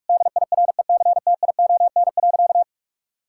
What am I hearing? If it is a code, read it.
Morse code, heard as a keyed tone, about 46 wpm, 700 Hz. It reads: DIRECTION.
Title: Sentences from Top 1000 Words - Review Part 5 - 45wpm